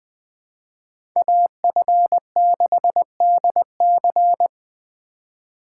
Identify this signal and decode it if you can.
Morse code, heard as a keyed tone, about 20 wpm, 700 Hz. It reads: AF6DC
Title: Speed Racing - USA 2x2 Call Signs 17wpm